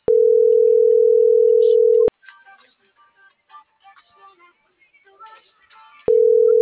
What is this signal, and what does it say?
Ringback tone (North America), 440 + 480 Hz.
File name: CENTRAL NMP / out itzel.arias